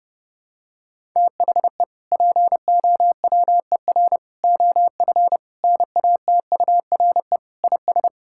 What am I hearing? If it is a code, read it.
Morse code, heard as a keyed tone, about 30 wpm, 700 Hz. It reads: THE POWER OF NATURE IS
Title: Sentences from Top 1000 Words - Review Part 5 - 30wpm